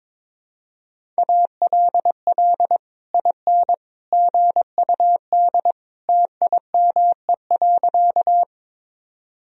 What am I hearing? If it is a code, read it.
Morse code, heard as a keyed tone, about 21 wpm, 700 Hz. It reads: ALL IN GUD TIME.
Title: Sentences from Top 200 Words and Abbreviations 22wpm